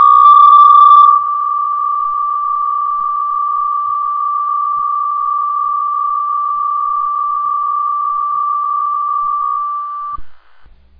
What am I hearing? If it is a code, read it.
PSK63, about 62.5 baud.